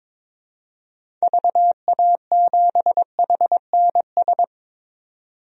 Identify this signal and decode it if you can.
Morse code, heard as a keyed tone, about 22 wpm, 700 Hz. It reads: VA7HNS
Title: DXCC-100 Call Signs - 6 Characters - 22wpm